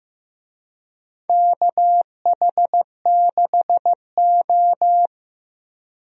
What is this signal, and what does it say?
Morse code, heard as a keyed tone, about 15 wpm, 700 Hz. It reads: KH6O
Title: DXCC-100 Call Signs - 4 Characters - 15wpm